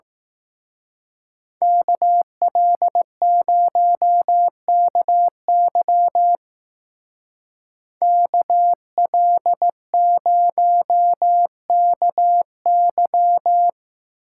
Morse code, heard as a keyed tone, about 17 wpm, 700 Hz. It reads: KL0KY KL0KY
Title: Speed Racing - USA 2x2 Call Signs 15wpm